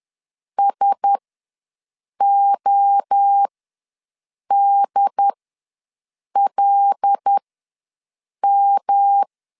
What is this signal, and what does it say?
Morse code, heard as a keyed tone, about 10 wpm, spaced as for 6 wpm, 800 Hz. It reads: SODLM
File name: LEZIONI / 5 LBVC / 5 L ws 10wpm 3636